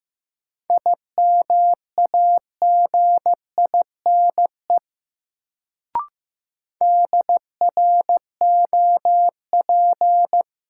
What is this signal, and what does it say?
Morse code, heard as a keyed tone, about 15 wpm, 700 Hz. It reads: IMAGINE DROP
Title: Sentences from Top 1400 Words - New Word Drill - 15wpm